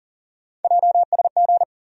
Morse code, heard as a keyed tone, about 40 wpm, 700 Hz. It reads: JSG
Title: USA Call Sign Postfixes - 3 Letters 40wpm